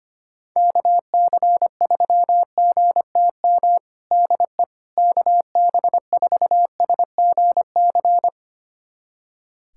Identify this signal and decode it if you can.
Morse code, heard as a keyed tone, about 25 wpm, 700 Hz. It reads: KC3GTM DE KB4SGC
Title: USA Callsign DE Callsign - 6 Characters - 25wpm